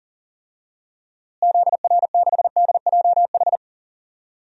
Morse code, heard as a keyed tone, about 40 wpm, 700 Hz. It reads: ZR6DJH
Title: DXCC-100 Call Signs - 6 Characters - Repeated 2x - 40wpm